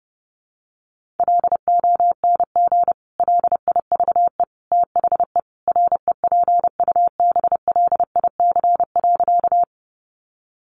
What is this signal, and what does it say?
Morse code, heard as a keyed tone, about 30 wpm, 700 Hz. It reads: LONG LIVE THE REPUBLIC.
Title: Sentences from Top 1500 Words 30wpm